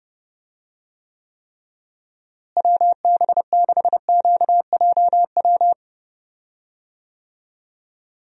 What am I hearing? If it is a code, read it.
Morse code, heard as a keyed tone, about 32 wpm, 700 Hz. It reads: WB6QJW
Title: USA Call Signs - 6 Characters - Repeated 3x - 30wpm